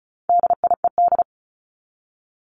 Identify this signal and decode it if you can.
Morse code, heard as a keyed tone, about 34 wpm, 700 Hz. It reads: DIED
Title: ED Suffix - Top 100 Words with Base Words 35wpm